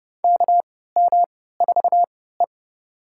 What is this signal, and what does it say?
Morse code, heard as a keyed tone, about 32 wpm, spaced as for 17 wpm, 700 Hz. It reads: KM4E
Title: USA Call Signs - 3x Character Spacing - 30wpm